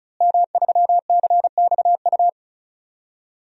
Morse code, heard as a keyed tone, about 36 wpm, 700 Hz. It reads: M3CXU
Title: DXCC-100 Call Signs - 5 Characters - Repeated 3x - 35wpm